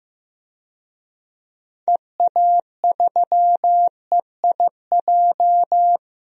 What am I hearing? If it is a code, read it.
Morse code, heard as a keyed tone, about 15 wpm, 700 Hz. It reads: EA3EIJ
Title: DXCC-100 Call Signs - 6 Characters - Repeated 3x - 15wpm